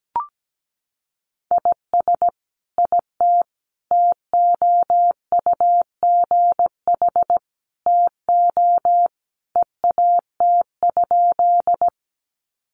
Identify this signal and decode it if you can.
Morse code, heard as a keyed tone, about 17 wpm, 700 Hz. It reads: IS IT TOUGH TO EAT?
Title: Sentences from Top 1200 Words 17wpm